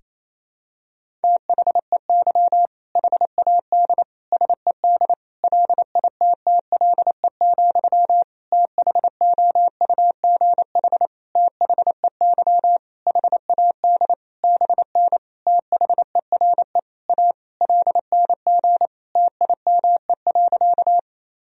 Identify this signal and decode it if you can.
Morse code, heard as a keyed tone, about 27 wpm, 700 Hz. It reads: THEY HAD SED LITTLE, THOUGH THEY HAD BN THERE A LNG TIME.
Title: Sentences from Top 400 Words and Abbreviations 28wpm